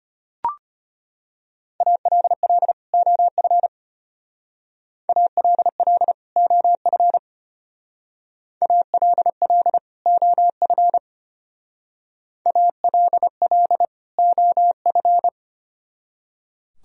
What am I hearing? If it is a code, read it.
Morse code, heard as a keyed tone, about 32 wpm, 700 Hz. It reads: ALL OF ALL OF ALL OF ALL OF